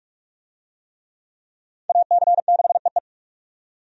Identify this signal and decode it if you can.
Morse code, heard as a keyed tone, about 45 wpm, 700 Hz. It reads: AC6EE
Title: USA Call Signs - 5 Characters - 45wpm